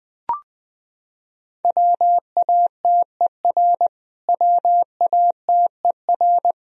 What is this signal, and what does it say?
Morse code, heard as a keyed tone, about 20 wpm, 700 Hz. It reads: WATER WATER
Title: Top 100 Words Repeated 20wpm